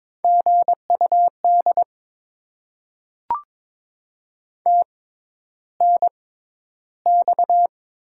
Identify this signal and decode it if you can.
Morse code, heard as a keyed tone, about 22 wpm, spaced as for 11 wpm, 700 Hz. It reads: GUD TNX
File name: Top 20 QSO Elements - 6x Character Spacing - 22wpm